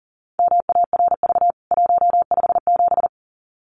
Morse code, heard as a keyed tone, about 40 wpm, 700 Hz. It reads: MARV 157
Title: CWT Exchange 40wpm